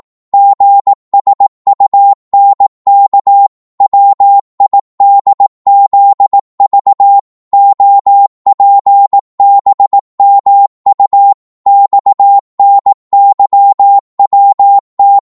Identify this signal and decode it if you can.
Morse code, heard as a keyed tone, about 18 wpm, 800 Hz. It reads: GSUNK WIDZV OPBMU XNYWT